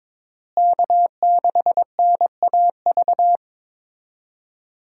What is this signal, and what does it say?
Morse code, heard as a keyed tone, about 22 wpm, 700 Hz. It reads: K6NAV
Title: USA Call Signs - 5 Characters - Repeated 3x - 22wpm